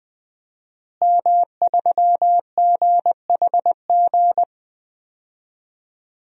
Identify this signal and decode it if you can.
Morse code, heard as a keyed tone, about 20 wpm, 700 Hz. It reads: M3GHG
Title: DXCC-100 Call Signs - 5 Characters - Repeated 2x - 20wpm